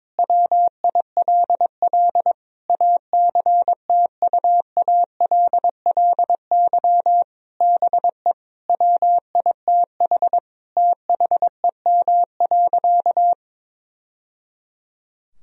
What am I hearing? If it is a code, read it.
Morse code, heard as a keyed tone, about 22 wpm, 700 Hz. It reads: WILL ACTUALLY BE WITH THEM.
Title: Sentences from Top 1000 Words - Review Part 4 - 22wpm